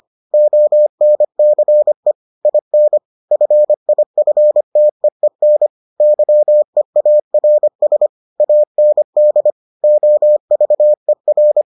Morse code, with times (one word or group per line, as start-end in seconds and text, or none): 0.3-2.1 ONCE
2.5-3.0 IN
3.3-5.7 FIFTEEN
6.0-8.1 YEARS
8.4-9.5 AND
9.8-11.6 OVER